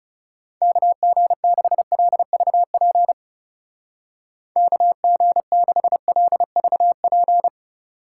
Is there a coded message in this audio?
Yes: Morse code from 0.6 to 7.5 s, heard as a keyed tone, about 39 wpm, 700 Hz. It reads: KG6LVP KG6LVP